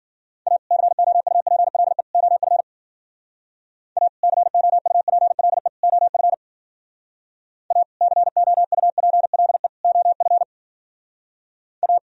Morse code, heard as a keyed tone, about 54 wpm, 700 Hz. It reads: A COUPLE OF A COUPLE OF A COUPLE OF A